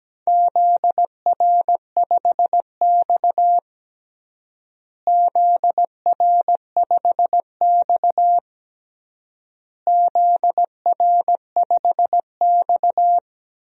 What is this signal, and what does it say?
Morse code, heard as a keyed tone, about 17 wpm, 700 Hz. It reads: ZR5X ZR5X ZR5X
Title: DXCC-100 Call Signs - 4 Characters - Repeated 3x - 17wpm